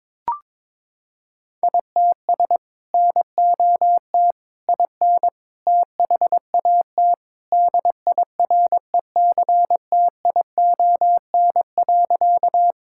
Morse code, heard as a keyed tone, about 22 wpm, 700 Hz. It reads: ITS NOT IN THAT DIRECTION.
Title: Sentences from Top 1000 Words - Review Part 5 - 22wpm